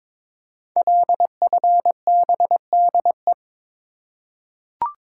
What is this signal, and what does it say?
Morse code, heard as a keyed tone, about 22 wpm, 700 Hz. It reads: LFBDE
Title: Code Groups - 5 Letters - 22wpm